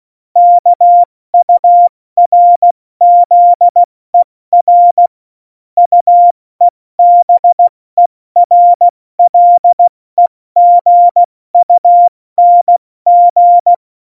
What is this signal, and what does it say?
Morse code, heard as a keyed tone, about 16 wpm, 700 Hz. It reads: KURZER UEBERLEGUNG
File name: Morsekurs / Witze / Witz 19. Januar 2021, 70 F80